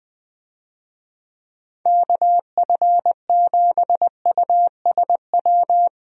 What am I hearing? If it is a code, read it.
Morse code, heard as a keyed tone, about 20 wpm, 700 Hz. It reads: KF7USW
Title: USA Call Signs - 6 Characters - 20wpm